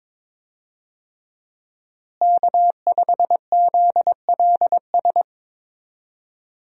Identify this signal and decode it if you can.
Morse code, heard as a keyed tone, about 22 wpm, 700 Hz. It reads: K5ZLS
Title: USA Call Signs - 5 Characters - Repeated 3x - 22wpm